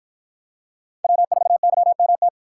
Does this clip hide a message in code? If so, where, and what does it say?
Yes: Morse code from 1.0 to 2.3 s, heard as a keyed tone, about 53 wpm, 700 Hz. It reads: W4YMT